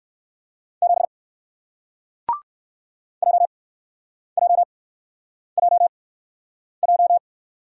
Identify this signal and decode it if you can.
Morse code, heard as a keyed tone, about 60 wpm, spaced as for 9 wpm, 700 Hz. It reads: 7 JJJJ